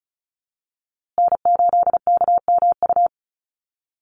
Morse code, heard as a keyed tone, about 34 wpm, 700 Hz. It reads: N8KMU
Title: USA Call Signs Repeated 2x 35wpm